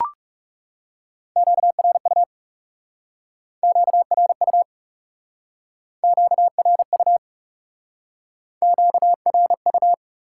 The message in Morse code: QRU QRU QRU QRU

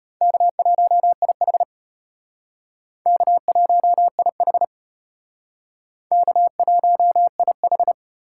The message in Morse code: K1IH K1IH K1IH

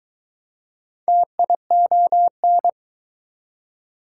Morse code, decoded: TION

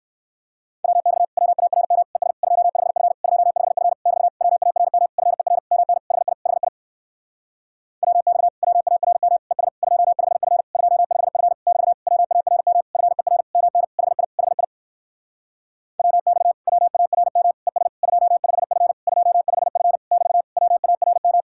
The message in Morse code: WX WARM ES 25F 25F <BT> WARM FER MN HI HI WX WARM ES 25F 25F <BT> WARM FER MN HI HI WX WARM ES 25F 25F <BT> WARM

